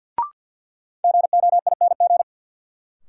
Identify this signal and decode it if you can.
Morse code, heard as a keyed tone, about 51 wpm, 700 Hz. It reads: GOING
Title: Sentences from Top 500 Words - New Word Drill - 50wpm